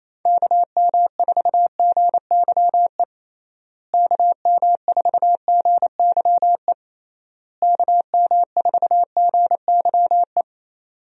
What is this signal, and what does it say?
Morse code, heard as a keyed tone, about 27 wpm, 700 Hz. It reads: KM4GYE KM4GYE KM4GYE
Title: USA Call Signs - 6 Characters - Repeated 3x - 28wpm